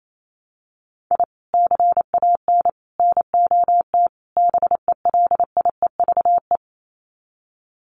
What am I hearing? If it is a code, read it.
Morse code, heard as a keyed tone, about 28 wpm, 700 Hz. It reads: I CAN NOT BELIEVE
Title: Sets of 4 Words 28wpm